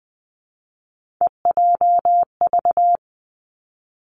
Morse code, heard as a keyed tone, about 19 wpm, 700 Hz. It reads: EJV
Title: Three Challenging Letter-Number Combinations 20wpm